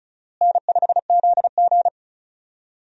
Morse code, heard as a keyed tone, about 34 wpm, 700 Hz. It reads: N5ZG
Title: USA Call Signs - 4 Characters - Repeated 2x - 35wpm